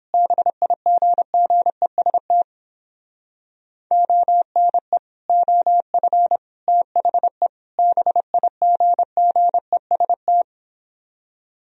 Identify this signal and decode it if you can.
Morse code, heard as a keyed tone, about 29 wpm, 700 Hz. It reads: BIGGEST ONE OF THE BIGGEST